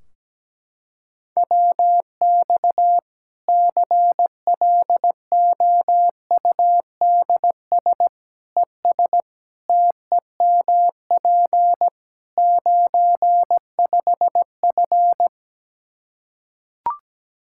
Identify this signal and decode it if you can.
Morse code, heard as a keyed tone, about 17 wpm, 700 Hz. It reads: WX CLOUDS ES TEMP 95F
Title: Practice QSO - Exchange 2 - Info - 17wpm